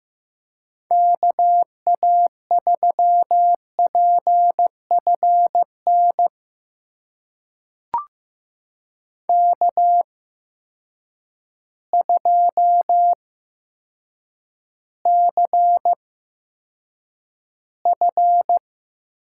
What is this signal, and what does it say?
Morse code, heard as a keyed tone, about 15 wpm, spaced as for 8 wpm, 700 Hz. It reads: KA3PFN K2CF
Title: USA Call Signs - 8x Character Spacing - 15wpm